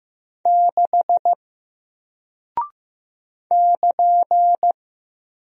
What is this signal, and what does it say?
Morse code, heard as a keyed tone, about 15 wpm, 700 Hz. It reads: 6 <KN>
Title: Single Character (Letter:Number:Punctuation:Prosign) Alt 15wpm